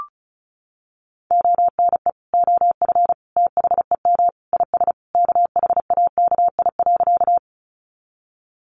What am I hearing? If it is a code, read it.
Morse code, heard as a keyed tone, about 34 wpm, 700 Hz. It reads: ONE OF THEM IS KHAKI.
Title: Additional Easy Phrases 35wpm